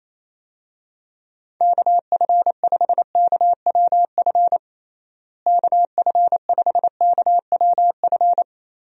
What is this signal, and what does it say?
Morse code, heard as a keyed tone, about 28 wpm, 700 Hz. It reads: KF5KWF KF5KWF
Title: USA Call Signs - 6 Characters - Repeated 2x - 28wpm